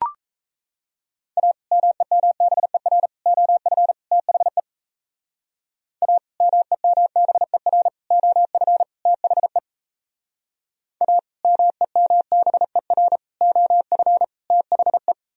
Morse code, heard as a keyed tone, about 37 wpm, 700 Hz. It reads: A MEMBER OF THE A MEMBER OF THE A MEMBER OF THE